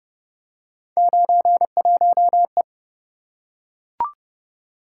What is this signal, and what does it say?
Morse code, heard as a keyed tone, about 27 wpm, 700 Hz. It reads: 91E